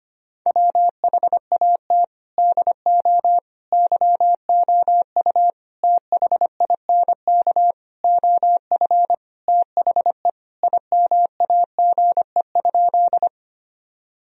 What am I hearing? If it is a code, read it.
Morse code, heard as a keyed tone, about 25 wpm, 700 Hz. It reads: WHAT DO YOU THINK OF THE IMAGE?
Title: Sentences from Top 1000 Words - Review Part 4 - 25wpm